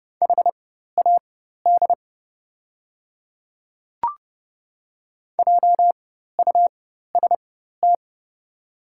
Morse code, heard as a keyed tone, about 30 wpm, spaced as for 14 wpm, 700 Hz. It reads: HAD JUST
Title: 4x Character Spacing - Top 300 Words - 30wpm